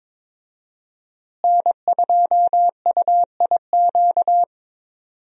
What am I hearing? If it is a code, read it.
Morse code, heard as a keyed tone, about 22 wpm, 700 Hz. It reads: N2UIQ
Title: USA Call Signs - 5 Characters - Repeated 2x - 22wpm